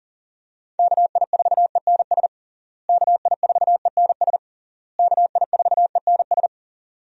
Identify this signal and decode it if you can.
Morse code, heard as a keyed tone, about 40 wpm, 700 Hz. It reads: KI4ENS KI4ENS KI4ENS
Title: USA Call Signs - 6 Characters - Repeated 3x - 40wpm